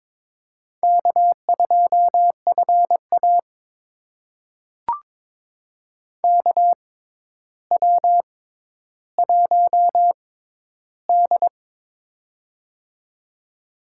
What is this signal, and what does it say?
Morse code, heard as a keyed tone, about 22 wpm, spaced as for 11 wpm, 700 Hz. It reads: K2FA KW1D
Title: USA Call Signs - 6x Character Spacing - 22wpm